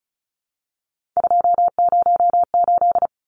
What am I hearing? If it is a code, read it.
Morse code, heard as a keyed tone, about 34 wpm, 700 Hz. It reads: 208
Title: Numbers - 3-Digit 35wpm